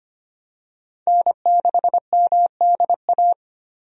Morse code, heard as a keyed tone, about 25 wpm, 700 Hz. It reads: N6MDA
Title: USA Call Signs - 5 Characters - 25wpm